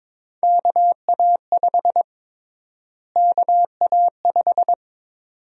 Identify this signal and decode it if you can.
Morse code, heard as a keyed tone, about 22 wpm, 700 Hz. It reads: KA5 KA5
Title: USA Call Sign Prefixes - Repeated 2x - 22wpm